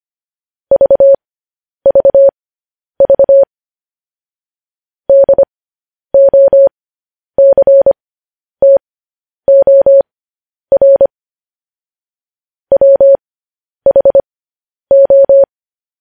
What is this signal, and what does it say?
Morse code, heard as a keyed tone, about 25 wpm, spaced as for 10 wpm, 550 Hz. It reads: VVV DOCTOR WHO
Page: news headlines in morse code at 10 wpm